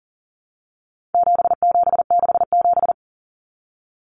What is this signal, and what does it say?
Morse code, heard as a keyed tone, about 40 wpm, 700 Hz. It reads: 7767
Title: Numbers - 4-Digit 40wpm